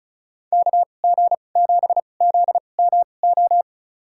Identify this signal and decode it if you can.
Morse code, heard as a keyed tone, about 34 wpm, spaced as for 25 wpm, 700 Hz. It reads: KG7ZMO
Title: USA Call Signs - 6 Characters - 2x Character Spacing - 35wpm